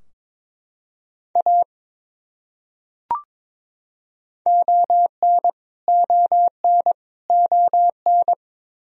Morse code, heard as a keyed tone, about 22 wpm, 700 Hz. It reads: A ON ON ON
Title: Top 100 Words Repeated 22wpm